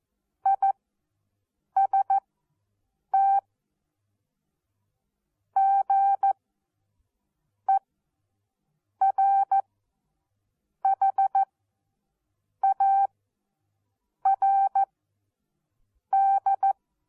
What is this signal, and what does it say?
Morse code, heard as a keyed tone, about 14 wpm, spaced as for 6 wpm, 780 Hz. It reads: IST GERHARD
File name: Onlinelehrgang / morsen / DJ4UF Morsen14